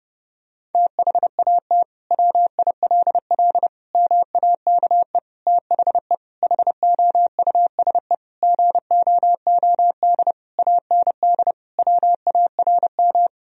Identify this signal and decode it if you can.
Morse code, heard as a keyed tone, about 30 wpm, 700 Hz. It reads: THAT WILL MAKE THE HOUSE GOOD AND WARM
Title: Sentences from Top 1000 Words - Review Part 3 - 30wpm